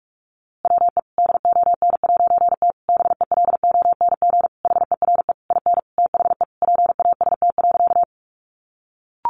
Morse code, heard as a keyed tone, about 44 wpm, 700 Hz. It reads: WE DON'T BELONG HERE IN THE PAST.
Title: A Sound Of Thunder 45wpm